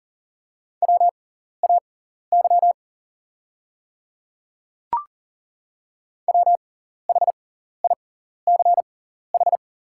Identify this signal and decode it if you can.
Morse code, heard as a keyed tone, about 40 wpm, spaced as for 14 wpm, 690 Hz. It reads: WAY WHICH
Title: 6x Character Spacing - Top 100 Words - 40wpm